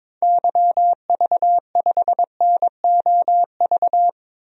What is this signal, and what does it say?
Morse code, heard as a keyed tone, about 22 wpm, 700 Hz. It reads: YV5NOV
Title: DXCC-100 Call Signs - 6 Characters - 22wpm